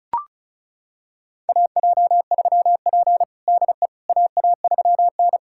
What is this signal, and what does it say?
Morse code, heard as a keyed tone, about 36 wpm, 700 Hz. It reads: AJ3P DE AA3N
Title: USA Callsign DE Callsign - 4 Characters - 35wpm